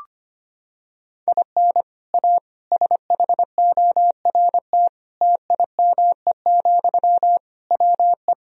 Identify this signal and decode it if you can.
Morse code, heard as a keyed tone, about 25 wpm, 700 Hz. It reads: IN A SHORT TIME, WE
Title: Sentences from Top 1000 Words - Review Part 2 - 25wpm